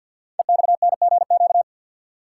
Morse code, heard as a keyed tone, about 45 wpm, 700 Hz. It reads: EXNGQ
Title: Code Groups - 5 Letters - 50wpm